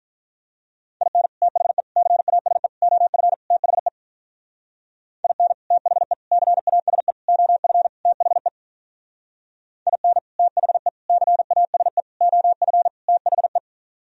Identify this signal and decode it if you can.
Morse code, heard as a keyed tone, about 42 wpm, 700 Hz. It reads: IN THE CASE OF THE IN THE CASE OF THE IN THE CASE OF THE
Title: Speed-Racing - Sets of 5 Words 35wpm